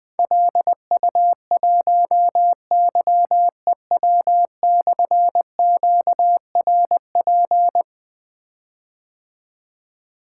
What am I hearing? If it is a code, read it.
Morse code, heard as a keyed tone, about 20 wpm, 700 Hz. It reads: LU1YEW/QRP
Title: Difficult Call Signs 20wpm